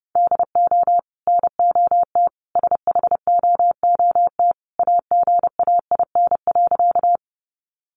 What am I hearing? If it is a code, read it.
Morse code, heard as a keyed tone, about 30 wpm, 700 Hz. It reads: DO NOT SHOOT AGAIN.
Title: Sentences from Top 1200 Words 30wpm